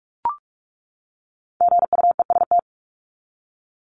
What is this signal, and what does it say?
Morse code, heard as a keyed tone, about 44 wpm, 700 Hz. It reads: GUEST
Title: Top 1501-1750 Words 45wpm